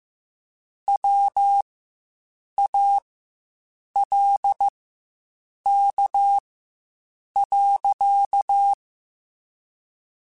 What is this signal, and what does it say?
Morse code, heard as a keyed tone, about 15 wpm, spaced as for 7 wpm, 800 Hz. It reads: WALK.
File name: Quotes Spaced / Quotes Spaced-mp3s / Temper sp @10